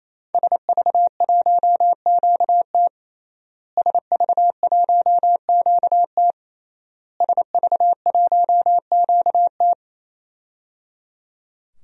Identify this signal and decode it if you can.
Morse code, heard as a keyed tone, about 28 wpm, 700 Hz. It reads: SV1QT SV1QT SV1QT
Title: DXCC-100 Call Signs - 5 Characters - Repeated 3x - 28wpm